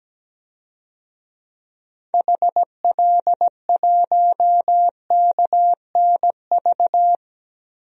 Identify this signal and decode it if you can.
Morse code, heard as a keyed tone, about 17 wpm, 700 Hz. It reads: HL1KNV
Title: DXCC-100 Call Signs - 6 Characters - Repeated 3x - 17wpm